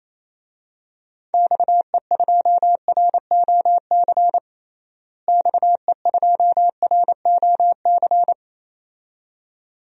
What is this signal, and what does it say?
Morse code, heard as a keyed tone, about 28 wpm, 700 Hz. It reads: XE2ROC XE2ROC
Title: DXCC-100 Call Signs - 6 Characters - Repeated 2x - 28wpm